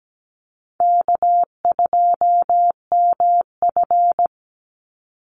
Morse code, heard as a keyed tone, about 17 wpm, 700 Hz. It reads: K2MF
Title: Easy Call Signs Repeated 3x - 17wpm